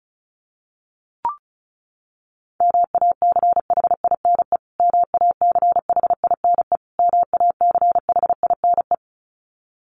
Morse code, heard as a keyed tone, about 34 wpm, 700 Hz. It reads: MACHINE MACHINE MACHINE
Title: Top 500 Words Repeated 35wpm